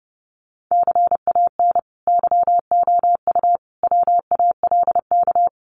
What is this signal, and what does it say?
Morse code, heard as a keyed tone, about 30 wpm, 700 Hz. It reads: CAN YOU WALK
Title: Sentences from Top 1000 Words 30wpm